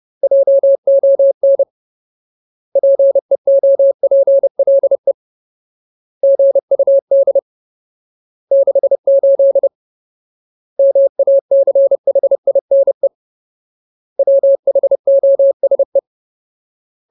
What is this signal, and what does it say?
Morse code, heard as a keyed tone, about 30 wpm, 550 Hz. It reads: JON PEOPLE GUD 68 MACHINE WHOSE
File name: Ultimate Word List / 30WPM Ultimate